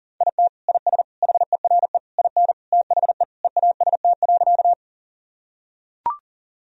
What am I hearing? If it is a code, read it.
Morse code, heard as a keyed tone, about 40 wpm, 700 Hz. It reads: IT IS HERE IN THE EAST.
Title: Sentences from Top 1000 Words - Review Part 3 - 40wpm